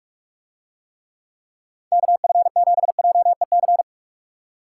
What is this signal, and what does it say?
Morse code, heard as a keyed tone, about 45 wpm, 700 Hz. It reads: KF7JEC